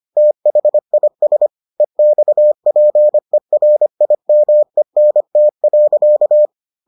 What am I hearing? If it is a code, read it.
Morse code, heard as a keyed tone, about 25 wpm, 600 Hz. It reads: THIS EXPERIMENT.